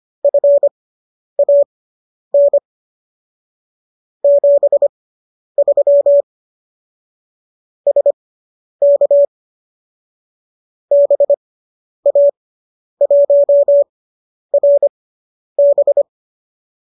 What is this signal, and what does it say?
Morse code, heard as a keyed tone, about 25 wpm, spaced as for 10 wpm, 570 Hz. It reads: FAN 73 SK BA1RB